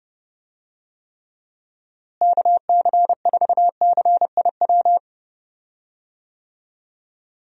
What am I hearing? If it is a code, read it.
Morse code, heard as a keyed tone, about 31 wpm, 700 Hz. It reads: KC4CIW
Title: USA Call Signs - 6 Characters - Repeated 2x - 30wpm